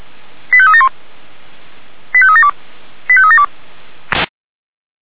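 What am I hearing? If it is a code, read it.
Five-tone selective call (ZVEI), heard as quick runs of five tones, tones of about 70 ms. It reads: 85371 85372 85372
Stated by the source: Five-tone ZVEI selective call received on an SDR.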